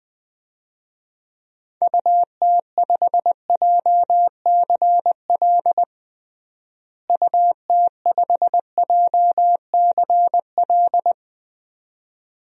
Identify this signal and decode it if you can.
Morse code, heard as a keyed tone, about 20 wpm, 700 Hz. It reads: UT5JCL UT5JCL
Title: DXCC-100 Call Signs - 6 Characters - Repeated 2x - 20wpm